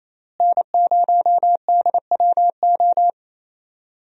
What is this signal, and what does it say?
Morse code, heard as a keyed tone, about 27 wpm, 700 Hz. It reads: N0DWO
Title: USA Call Signs - 5 Characters - Repeated 2x - 28wpm